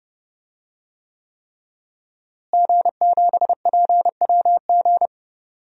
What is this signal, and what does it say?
Morse code, heard as a keyed tone, about 29 wpm, 700 Hz. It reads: G7PWG
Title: DXCC-100 Call Signs - 5 Characters - Repeated 3x - 30wpm